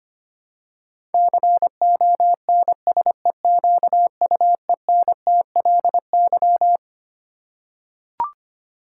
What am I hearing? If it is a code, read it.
Morse code, heard as a keyed tone, about 25 wpm, 700 Hz. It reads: CONSEQUENTLY
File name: Twelve Letter Words - 25wpm